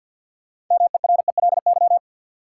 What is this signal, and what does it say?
Morse code, heard as a keyed tone, about 53 wpm, 700 Hz. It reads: MERELY
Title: Sentences from Top 1400 Words - New Word Drill Alt - 50wpm